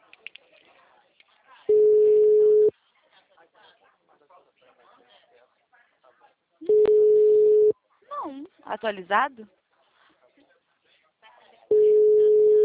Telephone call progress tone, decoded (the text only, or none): ringback tone